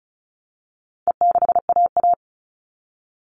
Morse code, heard as a keyed tone, about 34 wpm, 700 Hz. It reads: E6AA
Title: All Cut Numbers - 4-Digit 35wpm